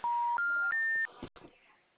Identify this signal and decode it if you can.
Special information tone.